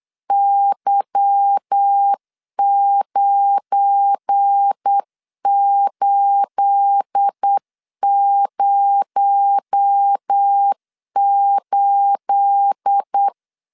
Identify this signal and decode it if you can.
Morse code, heard as a keyed tone, about 8 wpm, 800 Hz. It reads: Y9808
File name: LEZIONI / 9 28 8wpm 3512